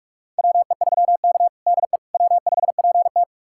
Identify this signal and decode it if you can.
Morse code, heard as a keyed tone, about 46 wpm, 700 Hz. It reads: WE3K DE W5PT
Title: USA Callsign DE Callsign - 4 Characters - 45wpm